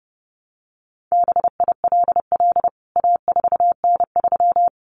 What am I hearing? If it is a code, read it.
Morse code, heard as a keyed tone, about 30 wpm, 700 Hz. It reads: BILL A4N3
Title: CWT Exchange 30wpm